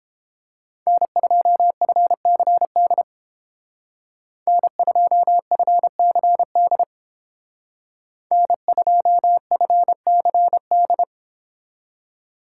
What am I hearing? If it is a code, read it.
Morse code, heard as a keyed tone, about 30 wpm, 700 Hz. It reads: N2FCD N2FCD N2FCD